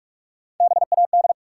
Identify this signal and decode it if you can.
Morse code, heard as a keyed tone, about 46 wpm, 700 Hz. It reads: BAD
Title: Sentences from Top 700 Words - New Word Drill - 45wpm